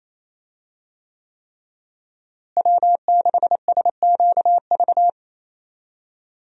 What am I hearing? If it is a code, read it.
Morse code, heard as a keyed tone, about 28 wpm, 700 Hz. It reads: W6SQV
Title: USA Call Signs - 5 Characters - Repeated 3x - 28wpm